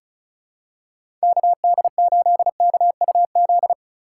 Morse code, heard as a keyed tone, about 36 wpm, 700 Hz. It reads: KD8KUZ
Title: USA Call Signs - 6 Characters - 2x Character Spacing - 35wpm